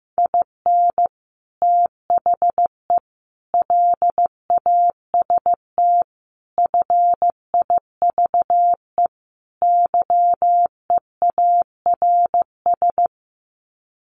Morse code, heard as a keyed tone, about 15 wpm, 700 Hz. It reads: IN THE LAST FIVE YEARS
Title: Sets of 5 Words Encore 15wpm